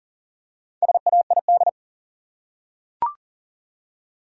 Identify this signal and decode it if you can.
Morse code, heard as a keyed tone, about 40 wpm, 700 Hz. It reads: SAID
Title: Top 100 Words Repeated 40wpm